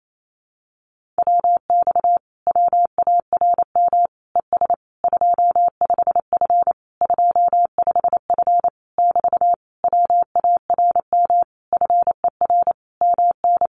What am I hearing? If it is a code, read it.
Morse code, heard as a keyed tone, about 28 wpm, 700 Hz. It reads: WX WARM ES 25F 25F <BT> WARM FER MN
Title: Common QSO Phrases Encore 28wpm